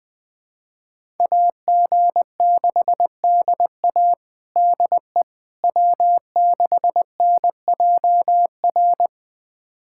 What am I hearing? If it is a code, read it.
Morse code, heard as a keyed tone, about 20 wpm, 700 Hz. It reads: AG6DA DE W6NJR
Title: USA Callsign DE Callsign - 5 Characters - 20wpm